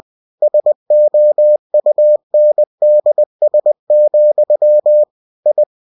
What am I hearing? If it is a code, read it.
Morse code, heard as a keyed tone, about 20 wpm, 600 Hz. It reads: SOUNDS, I
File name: cw-books / winnie-the-pooh / winnie-20wpm-page-0108